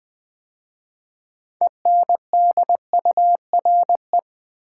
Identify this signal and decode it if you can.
Morse code, heard as a keyed tone, about 20 wpm, 700 Hz. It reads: ENDURE
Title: 6 Letter Words - First 3 Letters Given - 20wpm